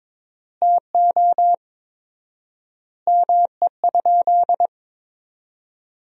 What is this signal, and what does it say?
Morse code, heard as a keyed tone, about 22 wpm, 700 Hz. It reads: TO ME?
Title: 4x Word-Spacing - Sentences from Top 200 Words - 22wpm